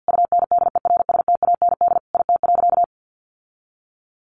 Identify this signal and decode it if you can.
Morse code, heard as a keyed tone, about 50 wpm, 700 Hz. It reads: UNDERSTAND IT.
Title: Sentences from Top 1400 Words 50wpm